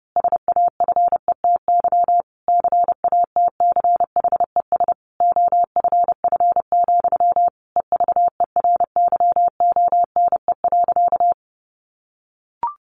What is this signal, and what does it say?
Morse code, heard as a keyed tone, about 30 wpm, 700 Hz. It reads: SAFETY CATCHES OFF, EVERYONE.
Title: A Sound Of Thunder 30wpm